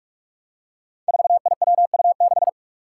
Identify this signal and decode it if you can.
Morse code, heard as a keyed tone, about 46 wpm, 700 Hz. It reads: 4IWU6
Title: Five Challenging Letter-Number Combinations 45wpm